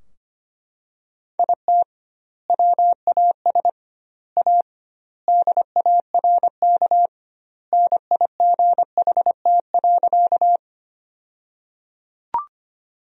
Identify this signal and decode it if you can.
Morse code, heard as a keyed tone, about 25 wpm, 700 Hz. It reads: IT WAS A DARK NIGHT.
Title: 2x Word-Spacing - Sentences from Top 500 Words - 25wpm